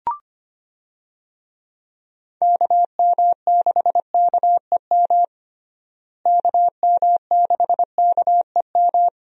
Morse code, heard as a keyed tone, about 25 wpm, 700 Hz. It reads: KM6KEM KM6KEM
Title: USA Call Signs - 6 Characters - Repeated 2x - 25wpm